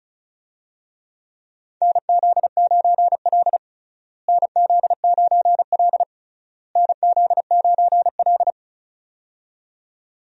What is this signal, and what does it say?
Morse code, heard as a keyed tone, about 36 wpm, 700 Hz. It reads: NZ9L NZ9L NZ9L
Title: USA Call Signs - 4 Characters - Repeated 3x - 35wpm